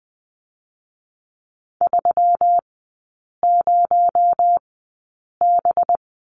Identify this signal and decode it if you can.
Morse code, heard as a keyed tone, about 20 wpm, spaced as for 8 wpm, 700 Hz. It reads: 30B
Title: 218 Lesson 35e - Number 0 - Three Characters 20wpm